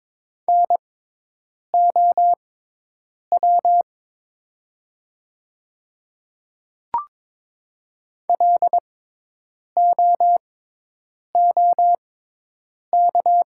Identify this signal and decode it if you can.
Morse code, heard as a keyed tone, about 22 wpm, spaced as for 8 wpm, 700 Hz. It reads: NOW LOOK
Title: 6x Character Spacing - Top 200 Words - 22wpm